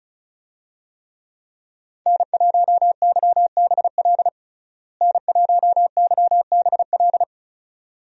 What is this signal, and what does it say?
Morse code, heard as a keyed tone, about 36 wpm, 700 Hz. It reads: N1YBL N1YBL
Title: USA Call Signs - 5 Characters - Repeated 2x - 35wpm